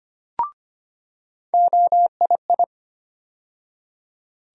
Morse code, heard as a keyed tone, about 25 wpm, 700 Hz. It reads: OII
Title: USA Call Sign Postfixes - 3 Letters 25wpm